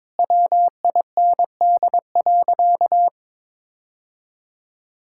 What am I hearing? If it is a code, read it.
Morse code, heard as a keyed tone, about 22 wpm, 700 Hz. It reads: WIND.
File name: practice / 22wpm / Sentences with 3-Word Phrases - 22wpm